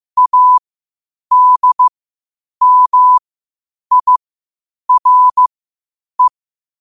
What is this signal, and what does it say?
Morse code, heard as a keyed tone, about 15 wpm, spaced as for 8 wpm, 1000 Hz. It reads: ADMIRE
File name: qotdfiles / 8wpm